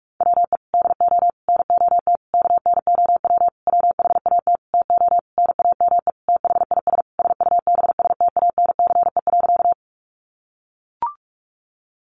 Morse code, heard as a keyed tone, about 44 wpm, 700 Hz. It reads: WE DO NOT KNOW WHAT TO NAME THIS SUBSTANCE.
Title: Sentences from Top 1500 Words 45wpm